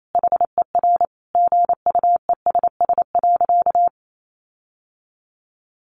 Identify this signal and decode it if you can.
Morse code, heard as a keyed tone, about 28 wpm, 700 Hz. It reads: HER GUESS.
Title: Sentences from Top 1500 Words 28wpm